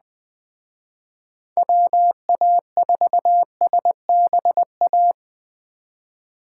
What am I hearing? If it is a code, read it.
Morse code, heard as a keyed tone, about 20 wpm, 700 Hz. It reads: WA4SBA
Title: Speed Racing - USA 2x3 Call Signs 17wpm